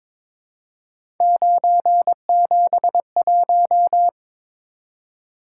Morse code, heard as a keyed tone, about 22 wpm, 700 Hz. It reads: 971
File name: practice / 22wpm / Three Letter-Number Combinations - 22wpm